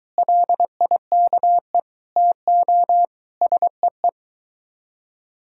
Morse code, heard as a keyed tone, about 23 wpm, 700 Hz. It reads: LIKE TO SEE